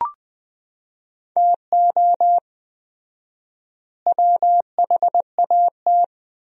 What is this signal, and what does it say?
Morse code, heard as a keyed tone, about 21 wpm, 700 Hz. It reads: TO WHAT
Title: 4x Word-Spacing - Sentences from Top 200 Words - 20wpm